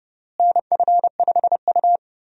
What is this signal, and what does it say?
Morse code, heard as a keyed tone, about 30 wpm, 700 Hz. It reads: NF5U
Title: USA Call Signs - 4 Characters - Repeated 3x - 30wpm